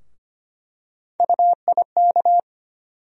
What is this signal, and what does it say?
Morse code, heard as a keyed tone, about 25 wpm, 700 Hz. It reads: UIK
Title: USA Call Sign Postfixes - 3 Letters 25wpm